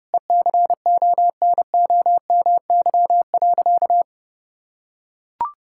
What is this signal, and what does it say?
Morse code, heard as a keyed tone, about 30 wpm, 700 Hz. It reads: ECONOMY.
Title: Sentences from Top 1000 Words - Review Part 5 - 30wpm